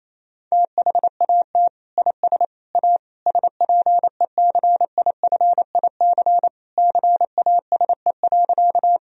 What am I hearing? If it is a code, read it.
Morse code, heard as a keyed tone, about 28 wpm, 700 Hz. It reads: THAT IS A SPECIFIC CASE.
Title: Sentences from Top 1000 Words - Review Part 3 - 28wpm